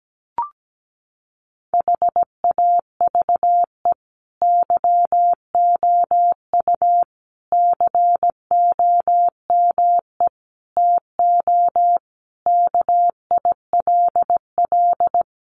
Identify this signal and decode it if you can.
Morse code, heard as a keyed tone, about 17 wpm, 700 Hz. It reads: HAVE YOU COME TO KILL
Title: Sentences from Top 1400 Words 17wpm